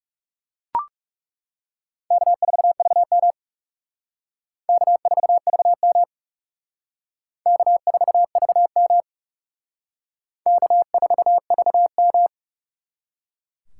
Morse code, heard as a keyed tone, about 39 wpm, 700 Hz. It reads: K4VM K4VM K4VM K4VM